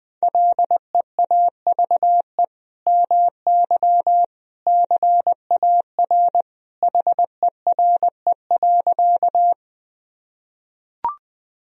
Morse code, heard as a keyed tone, about 21 wpm, 700 Hz. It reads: LEAVE MY CAR HERE.
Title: Sentences from Top 1000 Words - Review Part 1 - 20wpm